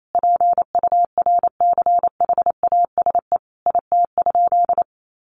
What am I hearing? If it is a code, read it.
Morse code, heard as a keyed tone, about 28 wpm, 700 Hz. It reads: PURCHASE IT?
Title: Sentences from Top 800 Words 28wpm